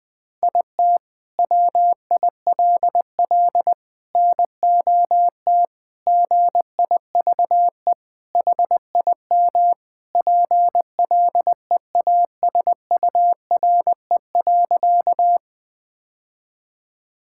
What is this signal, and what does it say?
Morse code, heard as a keyed tone, about 20 wpm, 700 Hz. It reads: IT WILL NOT GIVE HIM PLEASURE.
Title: Sentences from Top 1600 Words 20wpm